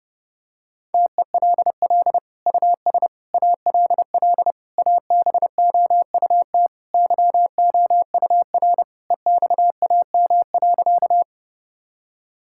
Morse code, heard as a keyed tone, about 30 wpm, 700 Hz. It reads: TELL US ALL ABOUT YOUR EXAM.
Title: Sentences from Top 1000 Words - Review Part 5 - 30wpm